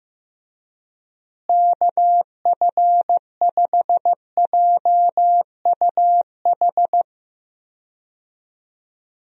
Morse code, heard as a keyed tone, about 15 wpm, 700 Hz. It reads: KF5JUH
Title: USA Call Signs - 6 Characters - Repeated 2x - 15wpm